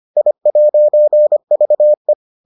Morse code, heard as a keyed tone, about 25 wpm, 600 Hz. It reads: I'VE